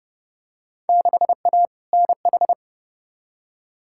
Morse code, heard as a keyed tone, about 31 wpm, 700 Hz. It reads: 6A NH
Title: Field Day Exchanges - 30wpm